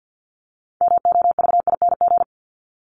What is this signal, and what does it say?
Morse code, heard as a keyed tone, about 50 wpm, 700 Hz. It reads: MOVING
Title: ING Suffix - Top 100 Words with Base Words 50wpm